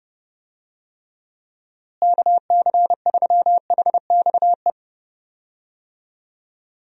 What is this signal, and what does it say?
Morse code, heard as a keyed tone, about 32 wpm, 700 Hz. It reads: KC3HXE
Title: USA Call Signs - 6 Characters - Repeated 3x - 30wpm